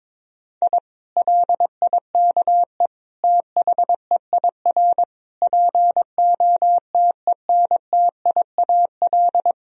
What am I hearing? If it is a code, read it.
Morse code, heard as a keyed tone, about 21 wpm, 700 Hz. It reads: I LIKE THEIR POTENTIAL
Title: Sentences from Top 1000 Words - Review Part 4 - 22wpm